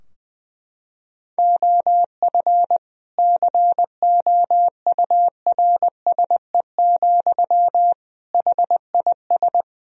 Morse code, heard as a keyed tone, about 20 wpm, 700 Hz. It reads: OF COURSE, HIS
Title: Sentences from Top 1000 Words - Review Part 3 - 20wpm